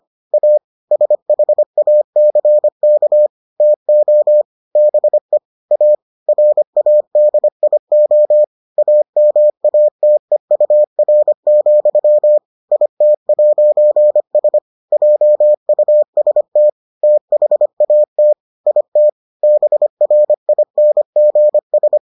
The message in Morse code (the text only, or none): A SHACK TO BE A RADIO AMATEUR, IT'S JUST THAT IT BRINGS